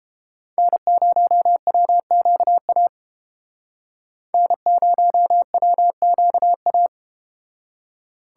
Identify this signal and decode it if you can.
Morse code, heard as a keyed tone, about 31 wpm, 700 Hz. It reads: N0WQA N0WQA